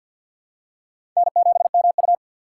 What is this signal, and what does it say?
Morse code, heard as a keyed tone, about 53 wpm, 700 Hz. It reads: N7MU
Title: USA Call Signs - 2x Character Spacing - 50wpm